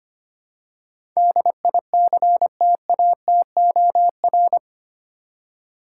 Morse code, heard as a keyed tone, about 25 wpm, 700 Hz. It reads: DICTATOR
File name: Eight Letter Words - 25wpm